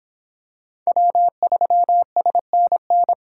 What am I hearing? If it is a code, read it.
Morse code, heard as a keyed tone, about 25 wpm, 700 Hz. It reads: W3SNN